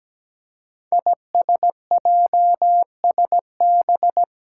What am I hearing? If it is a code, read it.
Morse code, heard as a keyed tone, about 17 wpm, 700 Hz. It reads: ISJSB
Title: Five Challenging Letter-Number Combinations 17wpm